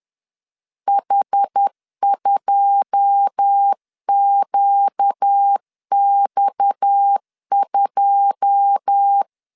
Morse code, heard as a keyed tone, about 10 wpm, 800 Hz. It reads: H2QX2
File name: LEZIONI / 9 2 10wpm 2736